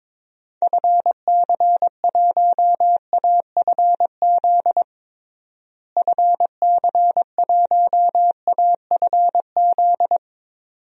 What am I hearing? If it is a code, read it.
Morse code, heard as a keyed tone, about 21 wpm, 700 Hz. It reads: FC1AFZ FC1AFZ
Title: DXCC-100 Call Signs - 6 Characters - Repeated 2x - 22wpm